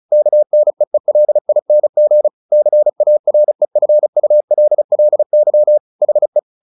Morse code, heard as a keyed tone, about 34 wpm, 600 Hz. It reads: KNEELING CAREFULLY HE
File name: the old man and the sea / the old man and the sea 1 35wpm